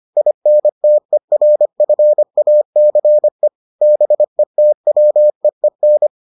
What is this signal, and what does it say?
Morse code, heard as a keyed tone, about 25 wpm, 600 Hz. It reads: INTERFACE BETWEEN